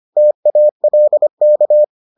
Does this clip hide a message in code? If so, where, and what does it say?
Yes: Morse code from 0.2 to 1.8 s, heard as a keyed tone, about 25 wpm, 600 Hz. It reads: TALK